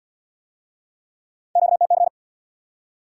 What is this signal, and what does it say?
Morse code, heard as a keyed tone, about 70 wpm, 700 Hz. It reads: QTC